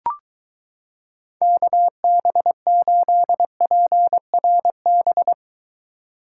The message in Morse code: KB8PRB

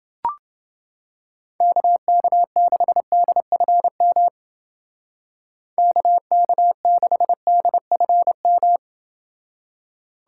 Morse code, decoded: KK6DFM KK6DFM

